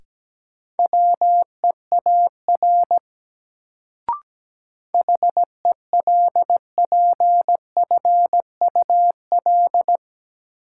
Morse code, heard as a keyed tone, about 17 wpm, 700 Hz. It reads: WEAR HELPFUL
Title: Sentences from Top 1000 Words - Review Part 6 - New Word Drill - 17wpm